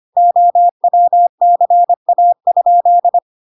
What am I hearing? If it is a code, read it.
Morse code, heard as a keyed tone, about 25 wpm, 700 Hz. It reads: OWCA?